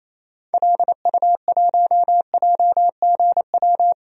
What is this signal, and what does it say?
Morse code, heard as a keyed tone, about 28 wpm, 700 Hz. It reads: LU1JGW
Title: DXCC-100 Call Signs - 6 Characters - 28wpm